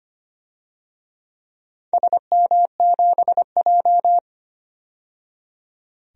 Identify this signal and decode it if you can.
Morse code, heard as a keyed tone, about 25 wpm, 700 Hz. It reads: SM7J
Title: DXCC-100 Call Signs - 4 Characters - 25wpm